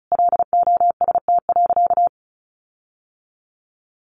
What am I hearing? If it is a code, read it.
Morse code, heard as a keyed tone, about 34 wpm, 700 Hz. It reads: LOST.
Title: Sentences from Top 1400 Words 35wpm